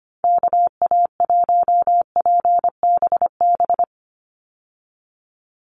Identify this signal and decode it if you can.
Morse code, heard as a keyed tone, about 25 wpm, 700 Hz. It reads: KA1PBB
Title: USA Call Signs 25wpm